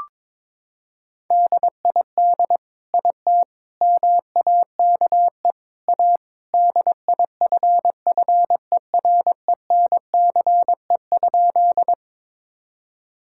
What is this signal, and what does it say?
Morse code, heard as a keyed tone, about 22 wpm, 700 Hz. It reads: DID IT MAKE A DIFFERENCE?
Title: Sentences from Top 1000 Words - Review Part 5 - 22wpm